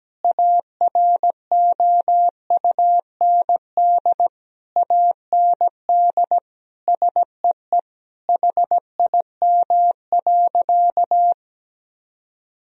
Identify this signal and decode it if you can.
Morse code, heard as a keyed tone, about 17 wpm, 700 Hz. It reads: AROUND AND SEE HIM.
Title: Sentences from Top 1000 Words - Review Part 3 - 17wpm